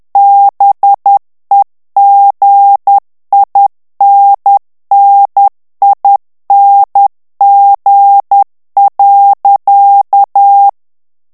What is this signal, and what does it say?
Morse code, heard as a keyed tone, about 10 wpm, 800 Hz. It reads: BEGINNING.
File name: morse mp3 / ANGLAIS texte clair 2 vitesse 10 mots minute